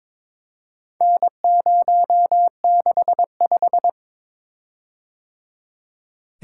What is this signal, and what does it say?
Morse code, heard as a keyed tone, about 22 wpm, 700 Hz. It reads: N065
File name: practice / 22wpm / Four Letter-Number Combinations - 22wpm